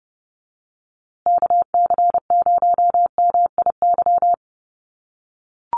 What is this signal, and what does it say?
Morse code, heard as a keyed tone, about 30 wpm, 700 Hz. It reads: KC0MIY
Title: USA Call Signs Repeated 3x 30wpm